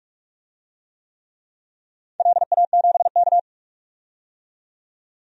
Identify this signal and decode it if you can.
Morse code, heard as a keyed tone, about 46 wpm, 700 Hz. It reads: LA7K
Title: DXCC-100 Call Signs - 4 Characters - 45wpm